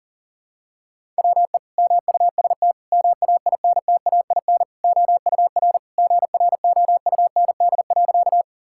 Morse code, heard as a keyed tone, about 40 wpm, 700 Hz. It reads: WE MUST MAINTAIN OUR GROUND.
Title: Sentences from Top 1000 Words - Review Part 5 - 40wpm